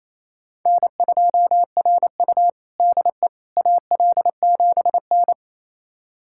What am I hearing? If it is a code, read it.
Morse code, heard as a keyed tone, about 28 wpm, 700 Hz. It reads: N2RU DE AL7N
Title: USA Callsign DE Callsign - 4 Characters - 28wpm